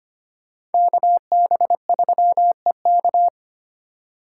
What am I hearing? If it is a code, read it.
Morse code, heard as a keyed tone, about 32 wpm, 700 Hz. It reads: KB3EK